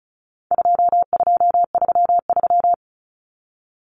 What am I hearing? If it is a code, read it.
Morse code, heard as a keyed tone, about 34 wpm, 700 Hz. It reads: 2233
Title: Common Cut Numbers - 4-Digit 35wpm